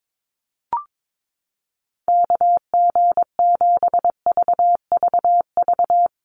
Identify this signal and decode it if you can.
Morse code, heard as a keyed tone, about 22 wpm, 690 Hz. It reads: KG7VVV
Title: USA Call Signs Repeated 3x 22wpm